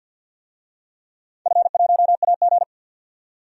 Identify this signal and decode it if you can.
Morse code, heard as a keyed tone, about 52 wpm, 700 Hz. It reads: F1AG
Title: DXCC-100 Call Signs - 4 Characters - 50wpm